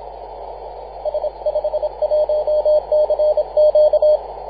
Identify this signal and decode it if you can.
Morse code, heard as a keyed tone, about 25 wpm, 610 Hz. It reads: S51CQ
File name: posts / 2015-oct-27-0935z-a35ok-14-010mhz-cw